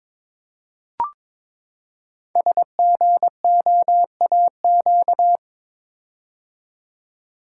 Morse code, heard as a keyed tone, about 22 wpm, 700 Hz. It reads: SGOAQ
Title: Code Groups - 5 Letters - 22wpm